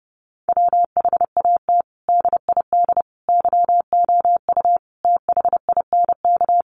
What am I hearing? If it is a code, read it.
Morse code, heard as a keyed tone, about 30 wpm, 700 Hz. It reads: WHAT DID YOU THINK
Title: Sets of 5 Words Encore2 30wpm